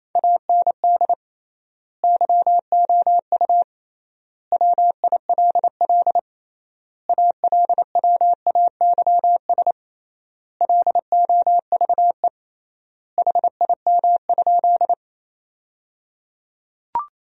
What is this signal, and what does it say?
Morse code, heard as a keyed tone, about 28 wpm, 700 Hz. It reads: AND YOU WILL ALWAYS LOVE HIM?
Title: 3x Word-Spacing - Sentences from Top 400 Words - 28wpm